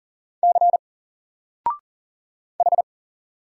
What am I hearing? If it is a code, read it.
Morse code, heard as a keyed tone, about 39 wpm, 700 Hz. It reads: C H
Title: Single Character (Letter:Number:Punctuation:Prosign) 40wpm